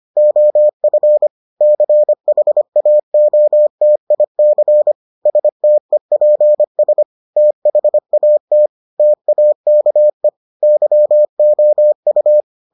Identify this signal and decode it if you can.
Morse code, heard as a keyed tone, about 25 wpm, 600 Hz. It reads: OF CHAOTIC STEPS THAT TAKE YOU